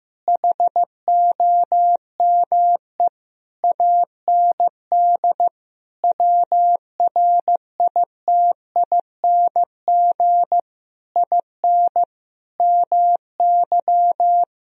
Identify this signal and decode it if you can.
Morse code, heard as a keyed tone, about 15 wpm, 700 Hz. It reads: HOME AND WRITING IN MY
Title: Sentences from Top 1000 Words - Review Part 1 - 15wpm